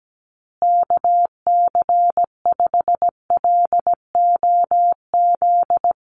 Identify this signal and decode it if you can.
Morse code, heard as a keyed tone, about 17 wpm, 700 Hz. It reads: KC5LOZ
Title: USA Call Signs Repeated 2x 17wpm